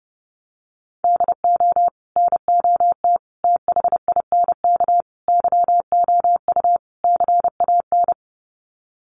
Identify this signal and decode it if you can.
Morse code, heard as a keyed tone, about 30 wpm, 700 Hz. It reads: DO NOT THINK YOU CAN
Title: Sets of 5 Words 30wpm